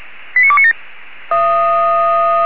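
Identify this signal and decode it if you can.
Five-tone selective call (ZVEI), heard as quick runs of five tones, tones of about 70 ms. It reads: 89297
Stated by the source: ZVEI five-tone selective call 89288 received via SDR, from the Feuerwehr Orlamünde (feuerwehr-orlamuende.org) audio server.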